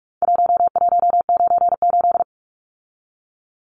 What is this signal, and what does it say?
Morse code, heard as a keyed tone, about 44 wpm, 700 Hz. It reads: 1198
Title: Numbers - Contest Style - 4-Digit 45wpm